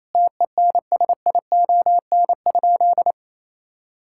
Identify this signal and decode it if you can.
Morse code, heard as a keyed tone, about 28 wpm, 700 Hz. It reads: TENSION?
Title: Sentences from Top 1600 Words 28wpm